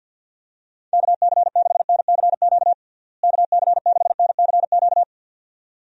Morse code, heard as a keyed tone, about 51 wpm, 700 Hz. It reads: KC6NCQ KC6NCQ
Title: USA Call Signs - 6 Characters - Repeated 2x - 50wpm